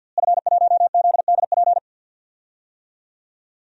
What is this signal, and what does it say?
Morse code, heard as a keyed tone, about 46 wpm, 700 Hz. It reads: F1ZDP
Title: DXCC-100 Call Signs - 5 Characters - Repeated 3x - 50wpm